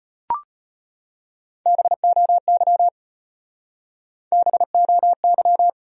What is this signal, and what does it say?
Morse code, heard as a keyed tone, about 30 wpm, 700 Hz. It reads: BOY BOY
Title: Speed-Racing Top 200 Words 25wpm